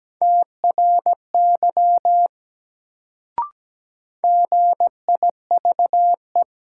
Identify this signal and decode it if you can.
Morse code, heard as a keyed tone, about 17 wpm, 700 Hz. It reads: TRY GIVE
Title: Sentences from Top 200 Words - New Word Drill - 17wpm